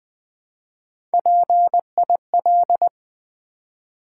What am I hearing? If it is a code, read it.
Morse code, heard as a keyed tone, about 20 wpm, 700 Hz. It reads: PIL
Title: Build A Word - 1 Letter At A Time - 4-Letter Words - 20wpm